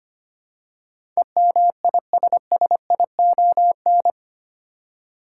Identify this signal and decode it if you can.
Morse code, heard as a keyed tone, about 25 wpm, 700 Hz. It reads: EMISSION
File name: Eight Letter Words - 25wpm